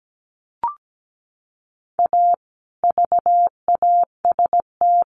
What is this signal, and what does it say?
Morse code, heard as a keyed tone, about 17 wpm, 700 Hz. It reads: A VAST
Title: Sentences from Top 1300 Words 17wpm